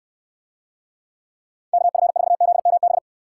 Morse code, heard as a keyed tone, about 68 wpm, 700 Hz. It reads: KF4ZMB